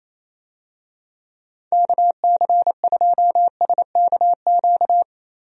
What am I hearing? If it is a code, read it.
Morse code, heard as a keyed tone, about 28 wpm, 700 Hz. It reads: KC2SKQ
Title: USA Call Signs - 6 Characters - 28wpm